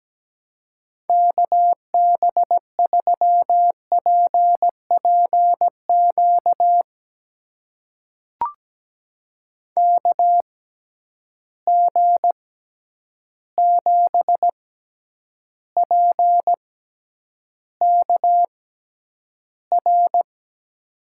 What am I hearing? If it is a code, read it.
Morse code, heard as a keyed tone, about 17 wpm, spaced as for 9 wpm, 700 Hz. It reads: KB3PPQ KG7PKR
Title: USA Call Signs - 6 Characters - 6x Character Spacing - 17wpm